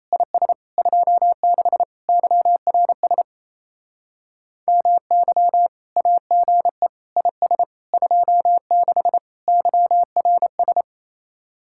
Morse code, heard as a keyed tone, about 34 wpm, 700 Hz. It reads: IS 26 YRS MY AGE IS 26 YRS
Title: Speed-Racing Common QSO Phrases - 28wpm